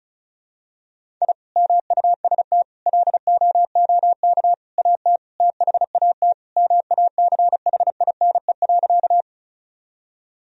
Morse code, heard as a keyed tone, about 36 wpm, 700 Hz. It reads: I MUST LOOK AT THAT MACHINE.
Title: Sentences from Top 1000 Words - Review Part 2 - 35wpm